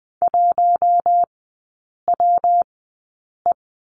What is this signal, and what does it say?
Morse code, heard as a keyed tone, about 20 wpm, spaced as for 8 wpm, 700 Hz. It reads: 1WE
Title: 191 Lesson 31e - Letter v - Three Characters 20wpm